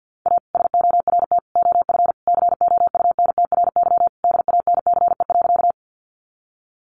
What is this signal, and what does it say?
Morse code, heard as a keyed tone, about 50 wpm, 700 Hz. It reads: A SORT OF COUNTRY DANCE.
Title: Sentences from Top 1300 Words 50wpm